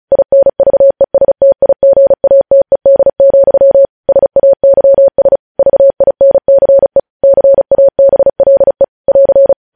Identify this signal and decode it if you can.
Morse code, heard as a keyed tone, about 36 wpm, 550 Hz. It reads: INVESTIGATED, SAYS VINCE CABLE <AR>